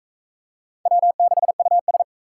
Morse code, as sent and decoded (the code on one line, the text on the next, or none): .-- -.... ..- ...
W6US